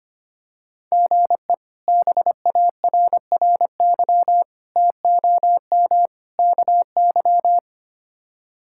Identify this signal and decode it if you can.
Morse code, heard as a keyed tone, about 25 wpm, 700 Hz. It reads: GE BARRY TOM KY
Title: K1USN SST QSO Practice - 25wpm